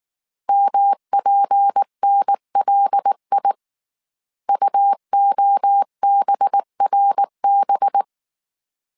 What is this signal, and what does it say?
Morse code, heard as a keyed tone, about 19 wpm, 800 Hz. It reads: MPNLI UOBRB
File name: CW / DET / 1 6 1500C S A 18wpm 2255